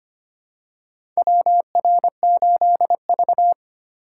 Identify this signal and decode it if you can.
Morse code, heard as a keyed tone, about 25 wpm, 700 Hz. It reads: WR8V
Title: USA Call Signs - 4 Characters - 3x Character Spacing - 25wpm